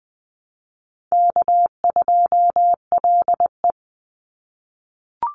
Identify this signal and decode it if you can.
Morse code, heard as a keyed tone, about 20 wpm, 700 Hz. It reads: K2LE
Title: Easy Call Signs - 20wpm